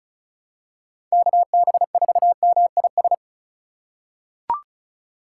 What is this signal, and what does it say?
Morse code, heard as a keyed tone, about 36 wpm, 700 Hz. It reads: KB4MIS
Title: USA Call Signs - 6 Characters - 3x Character Spacing - 35wpm